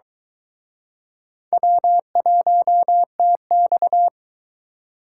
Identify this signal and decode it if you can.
Morse code, heard as a keyed tone, about 23 wpm, 700 Hz. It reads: W1TX